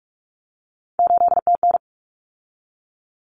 Morse code, heard as a keyed tone, about 44 wpm, 700 Hz. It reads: 8TN
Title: Common Cut Numbers - 3-Digit 45wpm